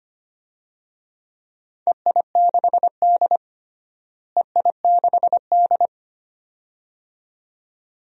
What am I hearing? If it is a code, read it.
Morse code, heard as a keyed tone, about 25 wpm, 700 Hz. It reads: EI6D EI6D
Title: DXCC-100 Call Signs - 4 Characters - Repeated 2x - 25wpm